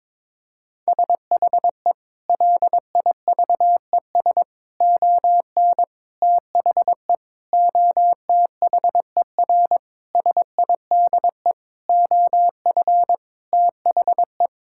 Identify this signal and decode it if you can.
Morse code, heard as a keyed tone, about 21 wpm, 700 Hz. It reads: SHE LIVES ON THE OTHER SIDE OF THE
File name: practice / 22wpm / Sentences with 7-Word Phrases - 22wpm